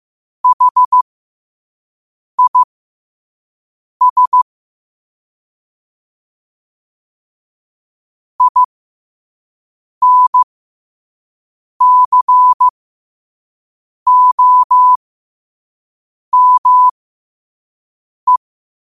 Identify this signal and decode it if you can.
Morse code, heard as a keyed tone, about 15 wpm, spaced as for 5 wpm, 1000 Hz. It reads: HIS INCOME
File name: qotdfiles / 5wpm